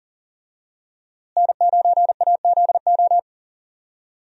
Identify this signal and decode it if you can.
Morse code, heard as a keyed tone, about 37 wpm, 700 Hz. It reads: N9AZO
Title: USA Call Signs - 5 Characters - Repeated 2x - 40wpm